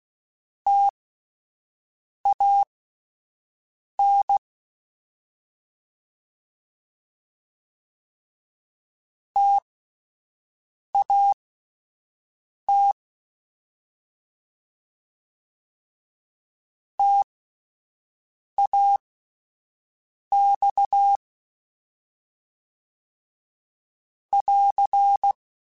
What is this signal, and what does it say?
Morse code, heard as a keyed tone, about 16 wpm, spaced as for 5 wpm, 800 Hz. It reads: TAN TAT TAX <AR>